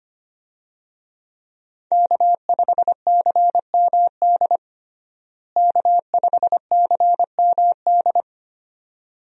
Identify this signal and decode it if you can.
Morse code, heard as a keyed tone, about 25 wpm, 700 Hz. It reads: K5CMD K5CMD
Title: USA Call Signs - 5 Characters - Repeated 2x - 25wpm